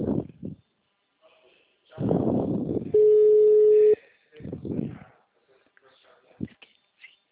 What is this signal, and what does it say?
Ringback tone (Europe, 425 Hz), with speech.